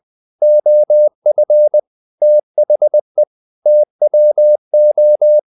Morse code, heard as a keyed tone, about 20 wpm, 600 Hz. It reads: OF THE TWO
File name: the old man and the sea / the old man and the sea part 1 20wpm